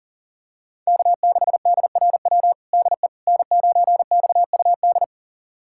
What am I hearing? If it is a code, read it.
Morse code, heard as a keyed tone, about 40 wpm, 700 Hz. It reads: K6DRW DE N9XUD
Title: USA Callsign DE Callsign - 5 Characters - 40wpm